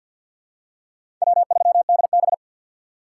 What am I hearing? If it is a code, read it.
Morse code, heard as a keyed tone, about 48 wpm, 700 Hz. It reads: W3DB
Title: USA Call Signs - 4 Characters - 6x Character Spacing - 50wpm